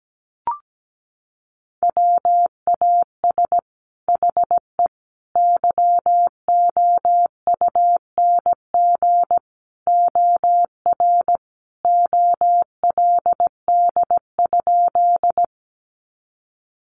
Morse code, heard as a keyed tone, about 17 wpm, 700 Hz. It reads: WAS HE YOUNG OR OLD?
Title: Sentences from Top 400 Words 17wpm